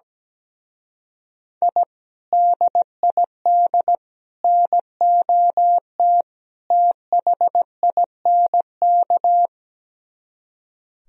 Morse code, heard as a keyed tone, about 17 wpm, 700 Hz. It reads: I DID NOT THINK
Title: Speed-Racing - Sets of 4 Words 17wpm